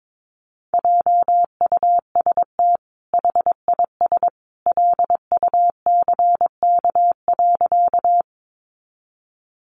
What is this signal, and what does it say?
Morse code, heard as a keyed tone, about 22 wpm, 700 Hz. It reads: JUST HIS LUCK.
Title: Sentences from Top 1500 Words 22wpm